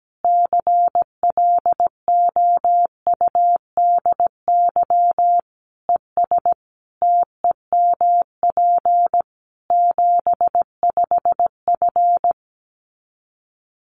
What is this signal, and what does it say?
Morse code, heard as a keyed tone, about 17 wpm, 700 Hz. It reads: CLOUDY ES TEMP 75F
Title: Common QSO Phrases 17wpm